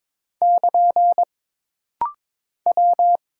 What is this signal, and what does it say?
Morse code, heard as a keyed tone, about 22 wpm, 700 Hz. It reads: <KN> W
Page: This rapid-fire timing is geared to push your ICR (Instant Character Recognition) skills.